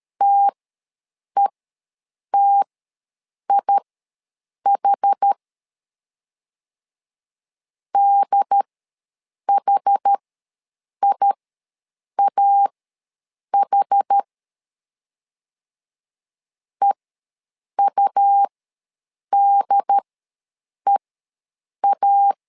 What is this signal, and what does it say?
Morse code, heard as a keyed tone, about 13 wpm, spaced as for 7 wpm, 800 Hz. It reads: TETIH DHIAH EUDEA
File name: LEZIONI / 2 UHDN / 2 UHD ws 12wpm 1722